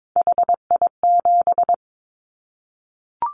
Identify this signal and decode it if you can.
Morse code, heard as a keyed tone, about 22 wpm, 700 Hz. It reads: HI7
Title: Callsign Prefixes for Top 100 DXCC - 22wpm